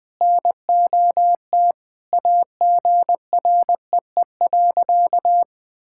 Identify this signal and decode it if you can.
Morse code, heard as a keyed tone, about 20 wpm, 700 Hz. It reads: NOT AGREE.
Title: Sentences with 4-Word Phrases - Spoken Beginning - 20wpm